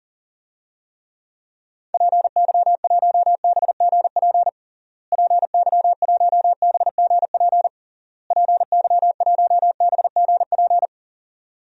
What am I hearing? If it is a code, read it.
Morse code, heard as a keyed tone, about 37 wpm, 700 Hz. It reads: PY1BGP PY1BGP PY1BGP
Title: DXCC-100 Call Signs - 6 Characters - Repeated 3x - 40wpm